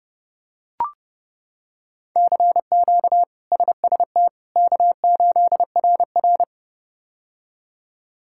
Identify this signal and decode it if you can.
Morse code, heard as a keyed tone, about 31 wpm, 700 Hz. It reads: CQ SST K8RR
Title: K1USN SST QSO Practice - 30wpm